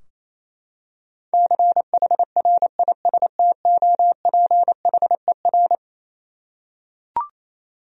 Morse code, heard as a keyed tone, about 28 wpm, 700 Hz. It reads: CHRISTOPHER
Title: USA Common Names - 28wpm